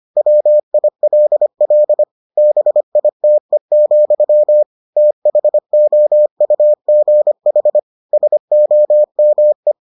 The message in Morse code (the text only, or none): WILL BITE, THOUGH SOME